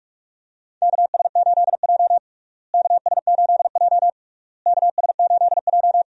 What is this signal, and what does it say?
Morse code, heard as a keyed tone, about 47 wpm, 700 Hz. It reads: KS8J KS8J KS8J
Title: USA Call Signs - 4 Characters - Repeated 3x - 45wpm